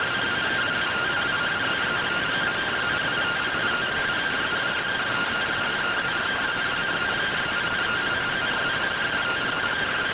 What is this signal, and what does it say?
Two-tone teleprinter signal, about 170 Hz shift.